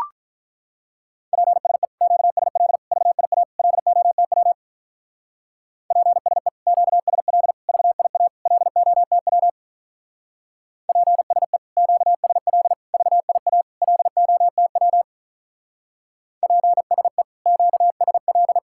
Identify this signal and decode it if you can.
Morse code, heard as a keyed tone, about 42 wpm, 700 Hz. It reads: PSE QSL VIA LOTW PSE QSL VIA LOTW PSE QSL VIA LOTW PSE QSL
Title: Speed-Racing Common QSO Phrases - 35wpm